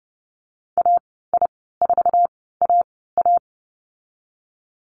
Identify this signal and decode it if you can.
Morse code, heard as a keyed tone, about 31 wpm, spaced as for 17 wpm, 700 Hz. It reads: AI4AA
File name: Lesson 19g - Letter w - Call Signs 30wpm